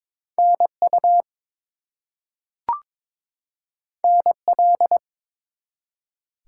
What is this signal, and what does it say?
Morse code, heard as a keyed tone, about 22 wpm, 700 Hz. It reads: NU NL
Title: US States and Canadian Provinces 22wpm